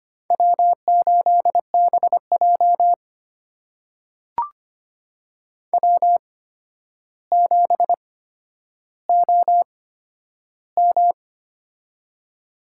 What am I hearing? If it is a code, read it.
Morse code, heard as a keyed tone, about 25 wpm, spaced as for 11 wpm, 700 Hz. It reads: W8BJ W7OM
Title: USA Call Signs - 4 Characters - 8x Character Spacing - 25wpm